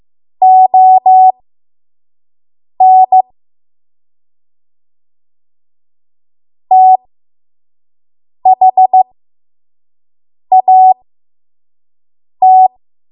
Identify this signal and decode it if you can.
Morse code, heard as a keyed tone, about 15 wpm, spaced as for 5 wpm, 750 Hz. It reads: ON THAT
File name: Morse / 05WPM